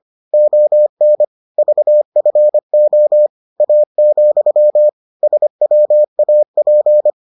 Morse code, heard as a keyed tone, about 25 wpm, 600 Hz. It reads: ON VFO A, SWAP